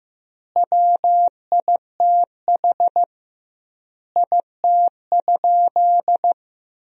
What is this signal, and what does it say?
Morse code, heard as a keyed tone, about 16 wpm, 700 Hz. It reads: WITH IT?
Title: 2x Word-Spacing - Sentences from Top 100 Words - 15wpm